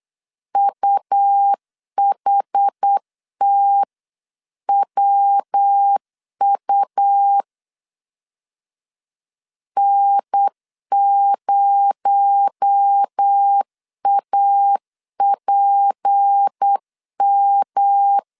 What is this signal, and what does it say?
Morse code, heard as a keyed tone, about 8 wpm, 800 Hz. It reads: UHTWU N0APM
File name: 8wpm